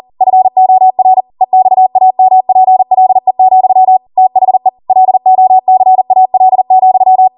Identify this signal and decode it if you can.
Morse code, heard as a keyed tone, about 40 wpm, 750 Hz. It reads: FOR EXAMPLE, THE LOCAL,
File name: Morse / 40WPM